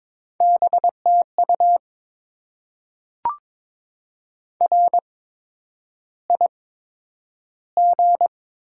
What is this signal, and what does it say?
Morse code, heard as a keyed tone, about 22 wpm, spaced as for 9 wpm, 700 Hz. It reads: BTU RIG
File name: Top 20 QSO Elements - 8x Character Spacing - 22wpm